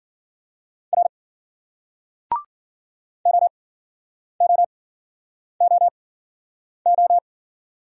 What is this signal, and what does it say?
Morse code, heard as a keyed tone, about 51 wpm, spaced as for 9 wpm, 700 Hz. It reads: R OOOO